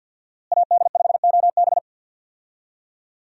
Morse code, heard as a keyed tone, about 52 wpm, 700 Hz. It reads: AD5OB
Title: USA Call Signs - 5 Characters - Repeated 2x - 50wpm